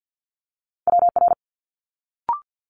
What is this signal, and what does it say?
Morse code, heard as a keyed tone, about 50 wpm, 700 Hz. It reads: WR
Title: Multi-letter Phonemes Quick Fire 50wpm